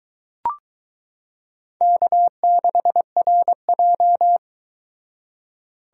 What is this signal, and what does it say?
Morse code, heard as a keyed tone, about 22 wpm, 700 Hz. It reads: K6RJ